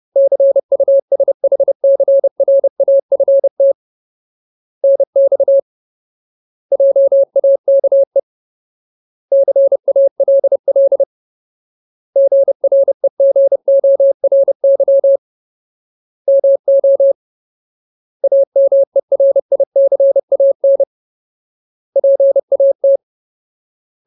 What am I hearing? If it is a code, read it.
Morse code, heard as a keyed tone, about 30 wpm, 550 Hz. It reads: CUSHCRAFT NX JAKE CALL GREGORY MO AMERICAN PAT
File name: Ultimate Word List / 30WPM Ultimate